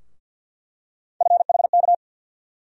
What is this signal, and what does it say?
Morse code, heard as a keyed tone, about 52 wpm, 690 Hz. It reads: FHK
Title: USA Call Sign Postfixes - 3 Letters 50wpm